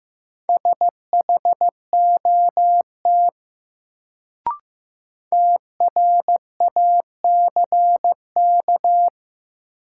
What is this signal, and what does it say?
Morse code, heard as a keyed tone, about 16 wpm, 700 Hz. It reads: SHOT TRACK
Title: Sentences from Top 1100 Words - New Word Drill - 15wpm